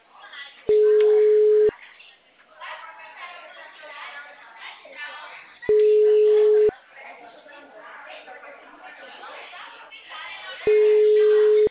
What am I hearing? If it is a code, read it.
Ringback tone (Europe, 425 Hz), with speech.